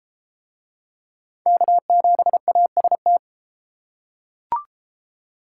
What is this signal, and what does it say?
Morse code, heard as a keyed tone, about 32 wpm, 700 Hz. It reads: K7AST